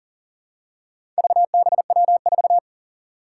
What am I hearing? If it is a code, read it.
Morse code, heard as a keyed tone, about 40 wpm, 700 Hz. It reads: VBW4
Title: Four Challenging Letter-Number Combinations 40wpm